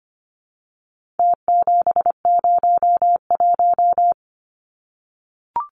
Morse code, heard as a keyed tone, about 25 wpm, 700 Hz. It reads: T701
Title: Numbers - Contest Style - 4-Digit 25wpm